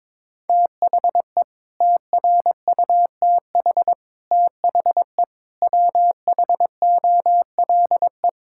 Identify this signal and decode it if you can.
Morse code, heard as a keyed tone, about 22 wpm, 700 Hz. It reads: THE TRUTH THE WHOLE
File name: Sets of 9 Words - 22wpm